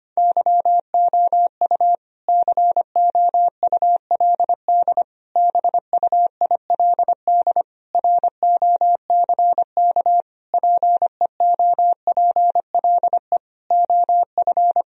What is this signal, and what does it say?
Morse code, heard as a keyed tone, about 25 wpm, 700 Hz. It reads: YOU COULD BUILD ROCK PEOPLE OF
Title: Sentences from Top 1000 Words - Review Part 2 - 25wpm